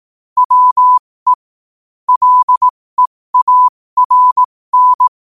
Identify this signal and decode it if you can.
Morse code, heard as a keyed tone, about 18 wpm, spaced as for 15 wpm, 1000 Hz. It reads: WE LEARN
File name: qotdfiles / 15wpm